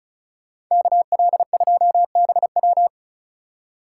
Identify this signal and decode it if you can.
Morse code, heard as a keyed tone, about 43 wpm, 700 Hz. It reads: KL2BW
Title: Speed Racing - USA 2x2 Call Signs 35wpm